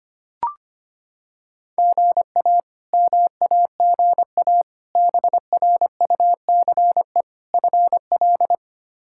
Morse code, heard as a keyed tone, about 25 wpm, 700 Hz. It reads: GA MAGA BRUCE FL
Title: K1USN SST QSO Practice - 25wpm